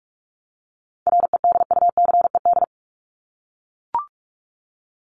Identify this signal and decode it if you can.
Morse code, heard as a keyed tone, about 44 wpm, 700 Hz. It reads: REDUCED
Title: ED Suffix - Top 100 Words with Base Words 45wpm